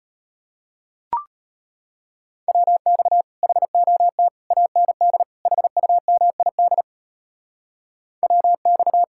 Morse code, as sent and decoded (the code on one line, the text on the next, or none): .-- -..- / .... --- - / .- -. -.. / .... ..- -- .. -.. / .-- -..-
WX HOT AND HUMID WX